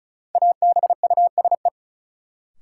Morse code, heard as a keyed tone, about 36 wpm, 700 Hz. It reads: ABUSE
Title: Sentences from Top 1300 Words - New Word Drill - 35wpm